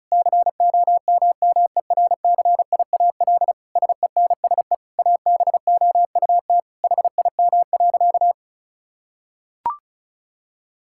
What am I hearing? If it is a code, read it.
Morse code, heard as a keyed tone, about 34 wpm, 700 Hz. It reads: COMMERCIAL SENSE ABOUT HIM.
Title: Sentences from Top 1000 Words - Review Part 5 - 35wpm